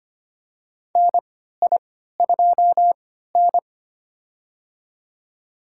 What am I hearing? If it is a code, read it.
Morse code, heard as a keyed tone, about 25 wpm, spaced as for 14 wpm, 700 Hz. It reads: NI2N
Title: USA Call Signs - 4 Characters - 3x Character Spacing - 25wpm